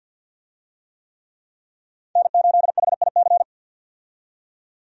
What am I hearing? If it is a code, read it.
Morse code, heard as a keyed tone, about 52 wpm, 700 Hz. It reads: N8HIC
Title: USA Call Signs - 5 Characters - 50wpm